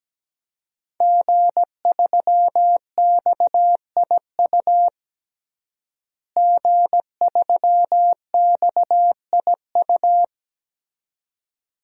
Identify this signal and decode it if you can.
Morse code, heard as a keyed tone, about 17 wpm, 700 Hz. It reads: G3XIU G3XIU
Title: DXCC-100 Call Signs - 5 Characters - Repeated 2x - 17wpm